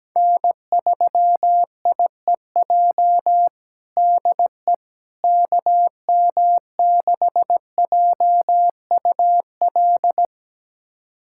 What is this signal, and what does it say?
Morse code, heard as a keyed tone, about 17 wpm, 700 Hz. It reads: N3IEJ DE KM6JUL
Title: USA Callsign DE Callsign 17wpm